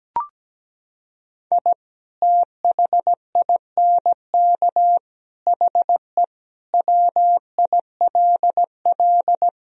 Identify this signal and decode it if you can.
Morse code, heard as a keyed tone, about 17 wpm, 700 Hz. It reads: I THINK HE WILL
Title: Sentences from Top 1000 Words - Review Part 3 - 17wpm